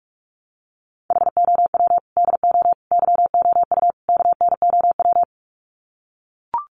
Morse code, heard as a keyed tone, about 44 wpm, 700 Hz. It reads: HOW DO YOU KNOW
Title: Sets of 4 Words 45wpm